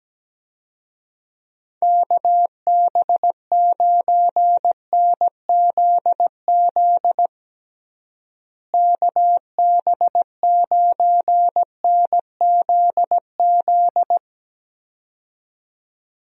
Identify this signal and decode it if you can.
Morse code, heard as a keyed tone, about 17 wpm, 700 Hz. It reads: KB9NZZ KB9NZZ
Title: USA Call Signs - 6 Characters - Repeated 2x - 17wpm